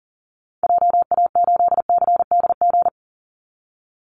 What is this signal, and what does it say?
Morse code, heard as a keyed tone, about 37 wpm, 700 Hz. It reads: JA8CDG
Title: Call Signs Repeated 2x - 40wpm